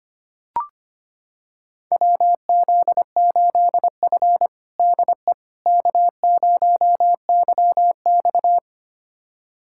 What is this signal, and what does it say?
Morse code, heard as a keyed tone, about 25 wpm, 700 Hz. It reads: WZ8F DE K0YX
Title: USA Callsign DE Callsign - 4 Characters - 25wpm